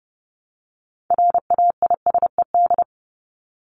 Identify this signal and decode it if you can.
Morse code, heard as a keyed tone, about 30 wpm, 700 Hz. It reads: RAISED
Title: ED Suffix - Top 100 Words with Base Words 30wpm